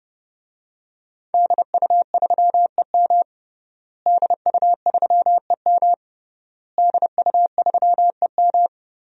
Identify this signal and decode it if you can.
Morse code, heard as a keyed tone, about 30 wpm, 700 Hz. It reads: DU3EM DU3EM DU3EM
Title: DXCC-100 Call Signs - 5 Characters - Repeated 3x - 30wpm